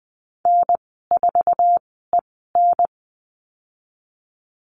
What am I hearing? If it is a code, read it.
Morse code, heard as a keyed tone, about 20 wpm, spaced as for 15 wpm, 700 Hz. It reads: N4EN
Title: 076 Lesson 13g - Letter L - Call Signs 20wpm